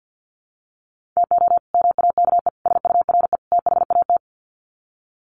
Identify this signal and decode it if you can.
Morse code, heard as a keyed tone, about 50 wpm, 700 Hz. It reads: TO MAKE SURE THAT
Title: Sets of 4 Words 50wpm